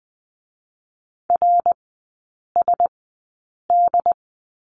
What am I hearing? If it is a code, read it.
Morse code, heard as a keyed tone, about 20 wpm, spaced as for 8 wpm, 700 Hz. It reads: RSD
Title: 107 Lesson 18e - Letter m - Three Characters 20wpm